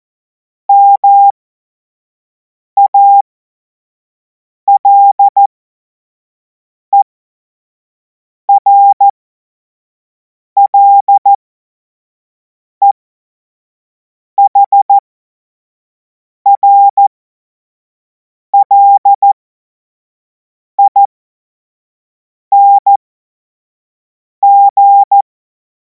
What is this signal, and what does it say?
Morse code, heard as a keyed tone, about 14 wpm, spaced as for 5 wpm, 800 Hz. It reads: MALERLEHRLING
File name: Morsekurs / Witze / Witz 10. Februar 2026 25 F70